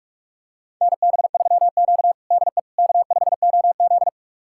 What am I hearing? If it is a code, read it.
Morse code, heard as a keyed tone, about 41 wpm, 700 Hz. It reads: NB3Q DE K5OZ
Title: USA Callsign DE Callsign - 4 Characters - 45wpm